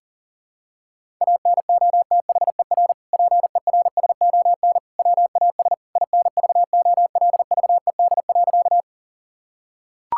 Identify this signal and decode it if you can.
Morse code, heard as a keyed tone, about 40 wpm, 700 Hz. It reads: ANOTHER PERSON WAS INVOLVED.
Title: Sentences from Top 1000 Words - Review Part 5 - 40wpm